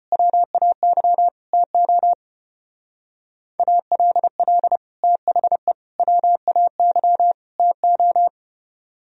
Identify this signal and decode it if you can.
Morse code, heard as a keyed tone, about 32 wpm, 700 Hz. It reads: WAY TO ALL THE WAY TO